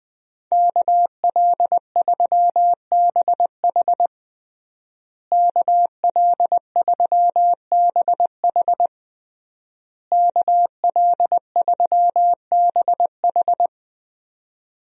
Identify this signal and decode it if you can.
Morse code, heard as a keyed tone, about 20 wpm, 700 Hz. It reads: KL3BH KL3BH KL3BH
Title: USA Call Signs - 5 Characters - Repeated 3x - 20wpm